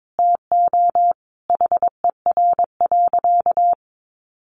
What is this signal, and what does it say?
Morse code, heard as a keyed tone, about 22 wpm, 700 Hz. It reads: TO HER.
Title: Sentences from Top 1100 Words 22wpm